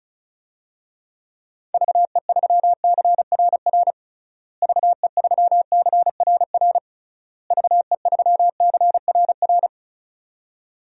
Morse code, heard as a keyed tone, about 36 wpm, 700 Hz. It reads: VE3CRR VE3CRR VE3CRR
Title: DXCC-100 Call Signs - 6 Characters - Repeated 3x - 35wpm